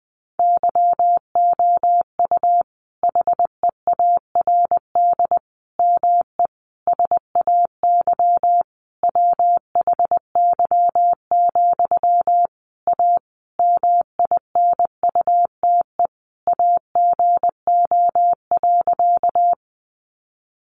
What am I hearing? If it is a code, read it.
Morse code, heard as a keyed tone, about 20 wpm, 700 Hz. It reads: YOU HEARD ME SAY WHY, A MINUTE AGO.
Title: Sentences from Top 500 Words 20wpm